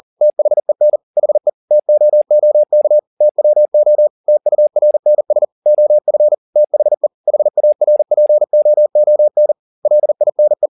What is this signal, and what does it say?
Morse code, heard as a keyed tone, about 40 wpm, 600 Hz. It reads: THEN HE TOOK TWO TURNS OF THE HARPOON LINE
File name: the old man and the sea 2 40wpm